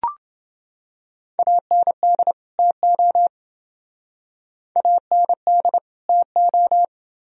Morse code, heard as a keyed tone, about 29 wpm, 700 Hz. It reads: AND TO AND TO